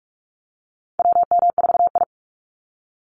Morse code, heard as a keyed tone, about 44 wpm, 700 Hz. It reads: WM4I
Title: Call Signs Repeated 3x - 45wpm